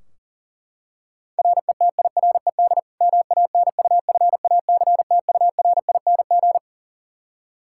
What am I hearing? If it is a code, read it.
Morse code, heard as a keyed tone, about 40 wpm, 700 Hz. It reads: RETIRED MANUFACTURING
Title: K1USN SST QSO Practice 40wpm